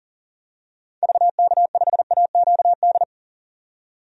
Morse code, heard as a keyed tone, about 38 wpm, 700 Hz. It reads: VK5AQD
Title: DXCC-100 Call Signs - 6 Characters - 40wpm